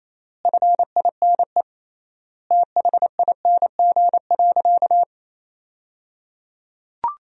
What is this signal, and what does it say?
Morse code, heard as a keyed tone, about 28 wpm, 700 Hz. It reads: FINE THING.
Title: 3x Word-Spacing - Sentences from Top 500 Words - 28wpm